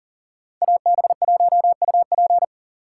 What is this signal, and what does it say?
Morse code, heard as a keyed tone, about 52 wpm, 700 Hz. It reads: AB1UP